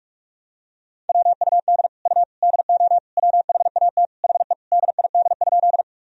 Morse code, heard as a keyed tone, about 46 wpm, 700 Hz. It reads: WUD U DO WHAT HE DID?
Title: Sentences from Top 100 Words and Abbreviations 45wpm